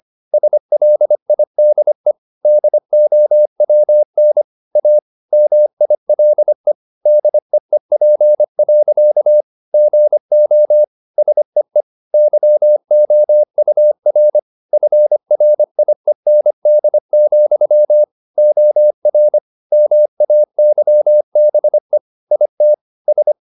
Morse code, heard as a keyed tone, about 25 wpm, 600 Hz. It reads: SLIDE DOWN A MILE DEEP. GO SEE YOUR FRIEND, OR MAYBE IT S